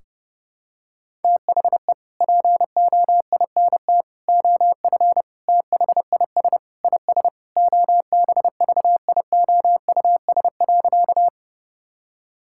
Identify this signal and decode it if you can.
Morse code, heard as a keyed tone, about 30 wpm, 700 Hz. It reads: THE POINT OF THIS IS OBVIOUS.
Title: Sentences from Top 1600 Words 30wpm